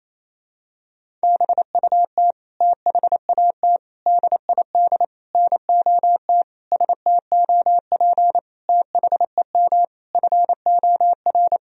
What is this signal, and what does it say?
Morse code, heard as a keyed tone, about 28 wpm, 700 Hz. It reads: BUT THAT DID NOT STOP THEM FOR
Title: Sentences from Top 1000 Words - Review Part 1 - 28wpm